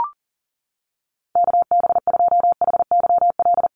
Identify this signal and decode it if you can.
Morse code, heard as a keyed tone, about 40 wpm, 700 Hz. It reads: KB2HYL
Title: USA Call Signs Repeated 2x 40wpm